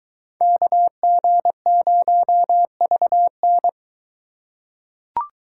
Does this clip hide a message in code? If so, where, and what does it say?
Yes: Morse code from 0.4 to 3.7 s, heard as a keyed tone, about 22 wpm, 700 Hz. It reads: KG0VN